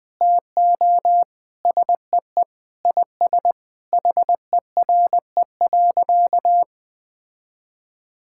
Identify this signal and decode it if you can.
Morse code, heard as a keyed tone, about 19 wpm, 700 Hz. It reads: TO SEE IS HERE.
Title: Sentences with 4-Word Phrases - Spoken Beginning - 20wpm